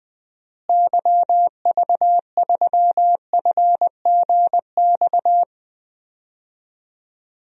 Morse code, heard as a keyed tone, about 20 wpm, 700 Hz. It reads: YV3FGX
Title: DXCC-100 Call Signs - 6 Characters - Repeated 2x - 20wpm